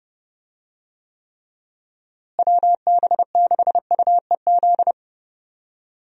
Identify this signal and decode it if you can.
Morse code, heard as a keyed tone, about 30 wpm, 700 Hz. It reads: WB6UEZ
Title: USA Call Signs - 6 Characters - Repeated 2x - 30wpm